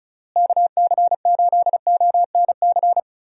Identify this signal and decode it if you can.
Morse code, heard as a keyed tone, about 36 wpm, 700 Hz. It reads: KC8ONC
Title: USA Call Signs - 6 Characters - Repeated 3x - 35wpm